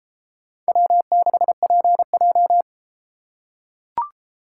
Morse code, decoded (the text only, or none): W6PJ